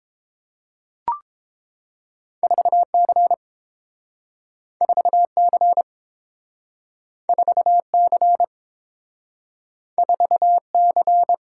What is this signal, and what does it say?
Morse code, heard as a keyed tone, about 32 wpm, 700 Hz. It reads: 4C 4C 4C 4C